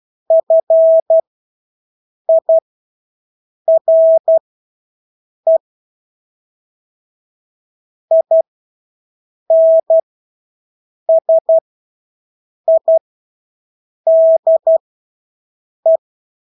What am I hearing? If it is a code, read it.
Morse code, heard as a keyed tone, about 12 wpm, spaced as for 6 wpm, 650 Hz. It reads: FIRE INSIDE
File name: Engelske ord 12-6 wpm